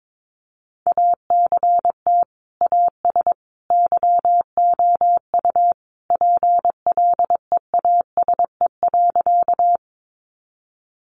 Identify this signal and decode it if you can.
Morse code, heard as a keyed tone, about 22 wpm, 700 Hz. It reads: ACT AS YOU PLEASE.
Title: Sentences from Top 1500 Words 22wpm